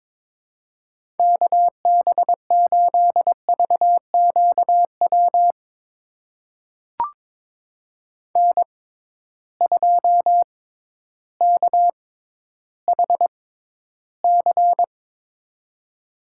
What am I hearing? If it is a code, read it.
Morse code, heard as a keyed tone, about 21 wpm, spaced as for 12 wpm, 700 Hz. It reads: KB8VQW N2KHC
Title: USA Call Signs - 6x Character Spacing - 22wpm